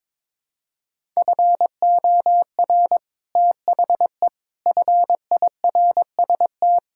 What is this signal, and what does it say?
Morse code, heard as a keyed tone, about 22 wpm, 700 Hz. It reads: FOR THE FIRST
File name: Sets of 9 Words - 22wpm